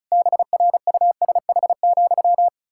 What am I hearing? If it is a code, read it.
Morse code, heard as a keyed tone, about 36 wpm, 700 Hz. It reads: BRUSH,
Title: Pangram Challenge v2 - 35wpm